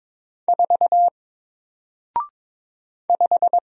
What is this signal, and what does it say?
Morse code, heard as a keyed tone, about 22 wpm, 700 Hz. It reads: 4 5
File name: Single-Letter-Number-22wpm